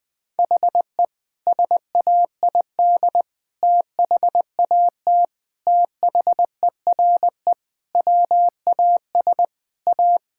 Morse code, heard as a keyed tone, about 21 wpm, 700 Hz. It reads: HE SAID THAT THERE WAS A
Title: Sentences with 3-Word Phrases - 20wpm